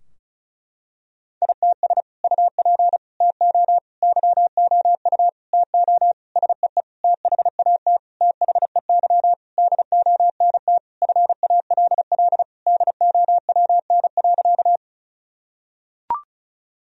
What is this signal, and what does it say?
Morse code, heard as a keyed tone, about 36 wpm, 700 Hz. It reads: ITS UP TO YOU TO SEE THAT THEY DONT FALL DOWN.
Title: Sentences from Top 1000 Words - Review Part 2 - 35wpm